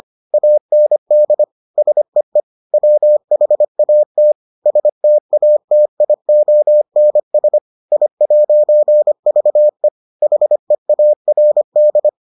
Morse code, heard as a keyed tone, about 25 wpm, 600 Hz. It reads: AND SEE WHAT STATIONS I'VE HEARD